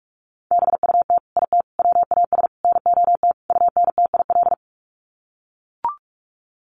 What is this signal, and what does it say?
Morse code, heard as a keyed tone, about 44 wpm, 700 Hz. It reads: BUT IT WAS NOT UNTIL
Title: Sets of 5 Words Encore2 45wpm